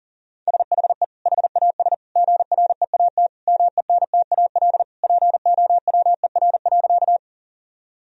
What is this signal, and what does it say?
Morse code, heard as a keyed tone, about 40 wpm, 700 Hz. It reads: SHE HAS GREAT MENTAL POWER.
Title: Sentences from Top 1000 Words - Review Part 4 - 40wpm